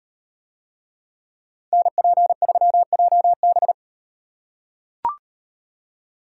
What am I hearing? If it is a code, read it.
Morse code, heard as a keyed tone, about 38 wpm, 700 Hz. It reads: NP3JB